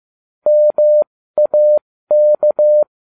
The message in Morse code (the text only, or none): MAK